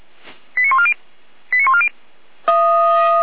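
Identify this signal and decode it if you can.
Five-tone selective call (ZVEI), heard as quick runs of five tones, tones of about 70 ms. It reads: 89140 89140
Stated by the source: ZVEI five-tone selective call received on an SDR, apparently at 455.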